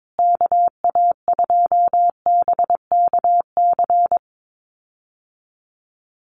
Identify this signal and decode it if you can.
Morse code, heard as a keyed tone, about 22 wpm, 690 Hz. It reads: KA2BKC
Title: USA Call Signs Repeated 3x 22wpm